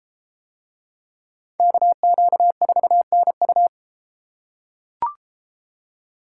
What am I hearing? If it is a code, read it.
Morse code, heard as a keyed tone, about 32 wpm, 700 Hz. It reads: KQ4NU